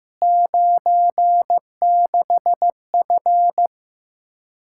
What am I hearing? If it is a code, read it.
Morse code, heard as a keyed tone, about 15 wpm, 700 Hz. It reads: 96F
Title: Practice QSO - Exchange 2 - Info - 15wpm